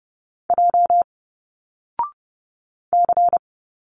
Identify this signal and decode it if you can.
Morse code, heard as a keyed tone, about 30 wpm, 700 Hz. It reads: J C
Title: Single Letters 30wpm